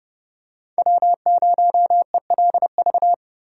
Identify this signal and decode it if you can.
Morse code, heard as a keyed tone, about 30 wpm, 700 Hz. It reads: W0ELV
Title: Code Groups - 5 Letter:Numbers - 30wpm